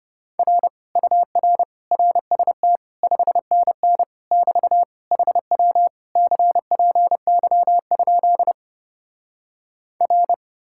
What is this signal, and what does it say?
Morse code, heard as a keyed tone, about 33 wpm, 700 Hz. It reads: R UR RST 5NN <BT> HW CPY? R